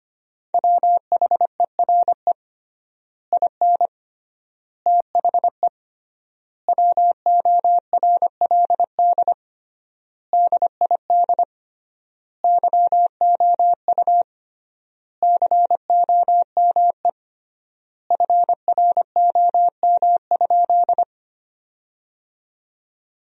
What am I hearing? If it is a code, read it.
Morse code, heard as a keyed tone, about 25 wpm, 700 Hz. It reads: WHERE IN THE WORLD DID YOU COME FROM?
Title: 3x Word-Spacing - Sentences from Top 200 Words - 25wpm